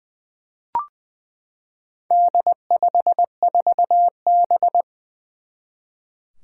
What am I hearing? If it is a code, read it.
Morse code, heard as a keyed tone, about 20 wpm, 700 Hz. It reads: D54B
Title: Four Challenging Letter-Number Combinations 20wpm